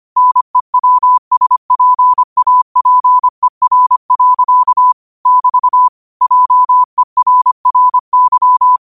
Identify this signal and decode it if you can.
Morse code, heard as a keyed tone, about 25 wpm, 1000 Hz. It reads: NEWSPAPER. <BT> JERRY
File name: qotdfiles / 25wpm